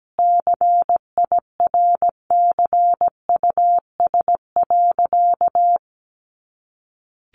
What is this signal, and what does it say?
Morse code, heard as a keyed tone, about 17 wpm, 700 Hz. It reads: CIRCUS.
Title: Easy Phrases 17wpm